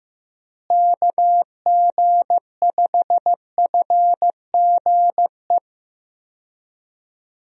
Morse code, heard as a keyed tone, about 15 wpm, 700 Hz. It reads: KG5FGE
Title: USA Call Signs - 6 Characters - Repeated 2x - 15wpm